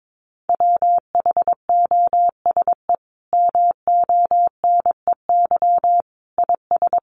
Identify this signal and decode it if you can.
Morse code, heard as a keyed tone, about 22 wpm, 700 Hz. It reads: WHOSE MONEY IS
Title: Sentences from Top 1200 Words 22wpm